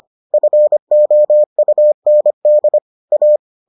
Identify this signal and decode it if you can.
Morse code, heard as a keyed tone, about 25 wpm, 600 Hz. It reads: FOUND A